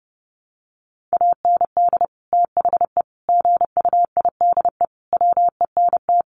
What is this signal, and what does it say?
Morse code, heard as a keyed tone, about 30 wpm, 700 Hz. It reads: AND THE GUIDE WENT
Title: Sentences from Top 600 Words 30wpm